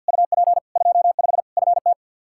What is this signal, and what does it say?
Morse code, heard as a keyed tone, about 50 wpm, 700 Hz. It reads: UP 25 FT
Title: Practice QSO - Exchange 2 - Info - 50wpm